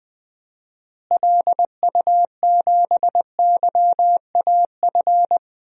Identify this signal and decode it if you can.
Morse code, heard as a keyed tone, about 20 wpm, 700 Hz. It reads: LU7YAF
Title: DXCC-100 Call Signs - 6 Characters - Repeated 3x - 20wpm